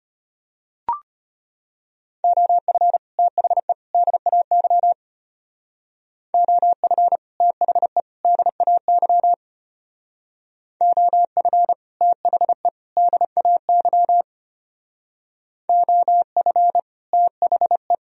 Morse code, heard as a keyed tone, about 31 wpm, 700 Hz. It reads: OF THE DAY OF THE DAY OF THE DAY OF THE
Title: Speed-Racing - Sets of 3 Words 25wpm